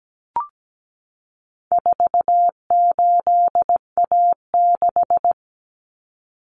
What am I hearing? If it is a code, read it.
Morse code, heard as a keyed tone, about 17 wpm, 700 Hz. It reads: 48A6
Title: Common Cut Numbers - 4-Digit 17wpm